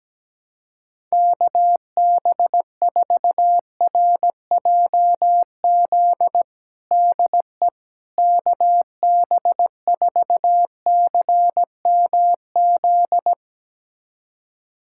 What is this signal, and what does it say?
Morse code, heard as a keyed tone, about 17 wpm, 700 Hz. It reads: KB4RJZ DE KB4CMZ
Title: USA Callsign DE Callsign - 6 Characters - 17wpm